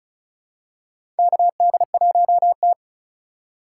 Morse code, heard as a keyed tone, about 34 wpm, 700 Hz. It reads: KD1T
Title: USA Call Signs - 4 Characters - 2x Character Spacing - 35wpm